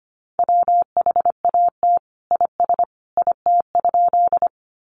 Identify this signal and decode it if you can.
Morse code, heard as a keyed tone, about 25 wpm, 700 Hz. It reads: WHAT IS IT?
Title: Sentences from Top 100 Words 25wpm